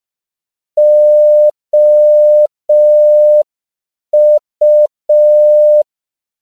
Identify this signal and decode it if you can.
Morse code, heard as a keyed tone, about 5 wpm, 600 Hz. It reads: OU
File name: CW / 5PPM